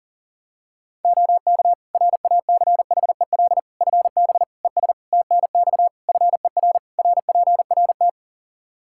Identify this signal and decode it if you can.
Morse code, heard as a keyed tone, about 40 wpm, 700 Hz. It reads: OK RACHEL FB ES TNX FER RPRT
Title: K1USN SST QSO Practice 40wpm